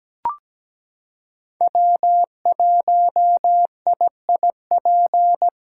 Morse code, heard as a keyed tone, about 17 wpm, 700 Hz. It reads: W1IIP
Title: USA Callsign DE Callsign - 5 Characters - 17wpm